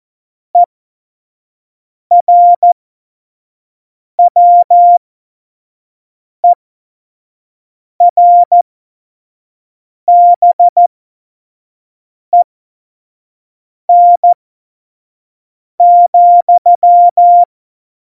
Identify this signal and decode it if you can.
Morse code, heard as a keyed tone, about 14 wpm, spaced as for 5 wpm, 700 Hz. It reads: ERWERBEN,
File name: Morsekurs / Witze / Witz 15. Dezember, F70